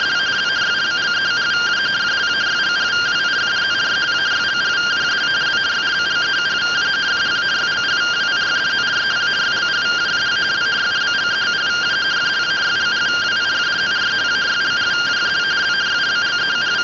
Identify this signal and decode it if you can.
Two-tone teleprinter signal, about 170 Hz shift.